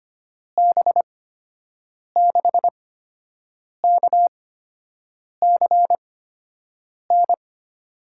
Morse code, heard as a keyed tone, about 25 wpm, spaced as for 7 wpm, 700 Hz. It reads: B6KCN
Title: USA Call Signs - 6 Characters - 8x Character Spacing - 25wpm